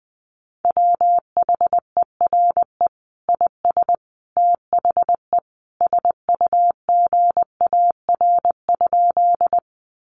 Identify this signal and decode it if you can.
Morse code, heard as a keyed tone, about 20 wpm, 700 Hz. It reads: WHERE IS THE SUGAR?
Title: Sentences from Top 1200 Words 20wpm